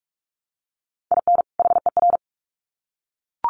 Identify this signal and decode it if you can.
Morse code, heard as a keyed tone, about 44 wpm, 700 Hz. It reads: IN HER
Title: Sets of 2 Words 45wpm